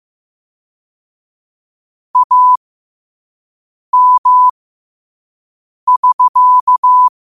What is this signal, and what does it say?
Morse code, heard as a keyed tone, about 15 wpm, spaced as for 5 wpm, 1000 Hz. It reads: AM<SK>
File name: qotdfiles / 5wpm